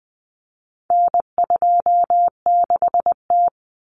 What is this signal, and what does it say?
Morse code, heard as a keyed tone, about 20 wpm, 700 Hz. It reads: N26T
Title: Common Cut Numbers - 4-Digit 20wpm